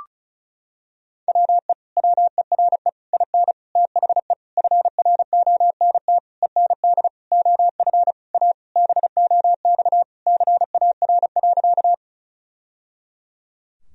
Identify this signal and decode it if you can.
Morse code, heard as a keyed tone, about 36 wpm, 700 Hz. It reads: WE WERE IN THE FRONT END OF A BOX CAR.
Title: Sentences from Top 1000 Words - Review Part 2 - 35wpm